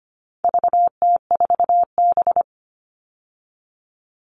Morse code, heard as a keyed tone, about 25 wpm, 700 Hz. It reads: VT4B
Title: All Cut Numbers - 4-Digit 25wpm